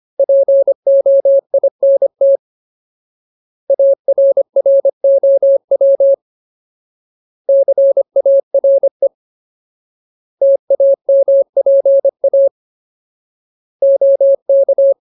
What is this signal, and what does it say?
Morse code, heard as a keyed tone, about 25 wpm, 550 Hz. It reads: POINT ARROW CARE TAMPA OK
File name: Ultimate Word List / 25WPM Ultimate